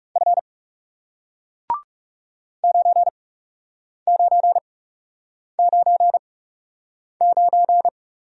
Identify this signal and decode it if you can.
Morse code, heard as a keyed tone, about 38 wpm, spaced as for 8 wpm, 700 Hz. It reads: F 9999